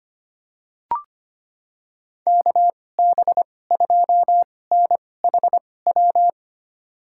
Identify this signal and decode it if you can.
Morse code, heard as a keyed tone, about 25 wpm, spaced as for 18 wpm, 700 Hz. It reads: KB2NHW
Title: USA Call Signs - 6 Characters - 2x Character Spacing - 25wpm